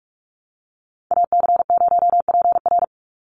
Morse code, heard as a keyed tone, about 44 wpm, 690 Hz. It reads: AC0PR
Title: NA SOTA Chaser Callsigns 45wpm